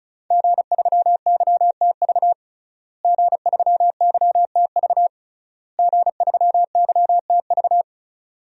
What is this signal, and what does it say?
Morse code, heard as a keyed tone, about 34 wpm, 700 Hz. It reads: G3YTV G3YTV G3YTV
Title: DXCC-100 Call Signs - 5 Characters - Repeated 3x - 35wpm